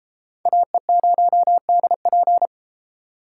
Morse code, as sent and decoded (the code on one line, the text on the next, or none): .- . ----- -.. .--.
AE0DP